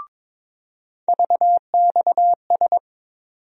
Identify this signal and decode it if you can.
Morse code, heard as a keyed tone, about 21 wpm, 700 Hz. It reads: VXS
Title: USA Call Sign Postfixes - 3 Letters 22wpm